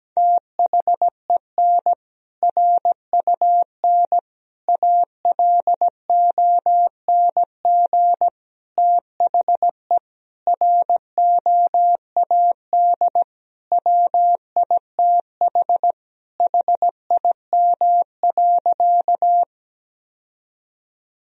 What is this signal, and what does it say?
Morse code, heard as a keyed tone, about 17 wpm, 700 Hz. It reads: THEN RUN ALONG THE ROAD WITH HIM.
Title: Sentences from Top 1000 Words - Review Part 2 - 17wpm